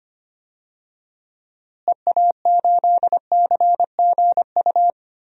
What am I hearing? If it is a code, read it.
Morse code, heard as a keyed tone, about 25 wpm, 700 Hz. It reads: EA8CGU
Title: DXCC-100 Call Signs - 6 Characters - Repeated 2x - 25wpm